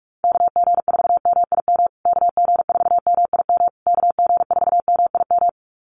Morse code, heard as a keyed tone, about 44 wpm, 700 Hz. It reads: KG4MIM KG4MIM KG4MIM
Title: USA Call Signs Repeated 3x 45wpm